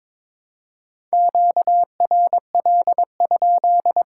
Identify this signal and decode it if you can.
Morse code, heard as a keyed tone, about 22 wpm, 700 Hz. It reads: QRL?
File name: Top 90 QSO Elements - 2x Character Spacing - 22wpm